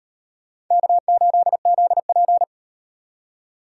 Morse code, heard as a keyed tone, about 38 wpm, 700 Hz. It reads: K8ZP